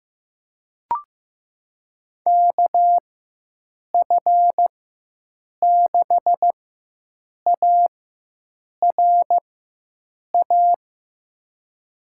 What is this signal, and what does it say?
Morse code, heard as a keyed tone, about 16 wpm, spaced as for 7 wpm, 700 Hz. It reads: KF6ARA
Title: USA Call Signs - 6 Characters - 4x Character Spacing - 15wpm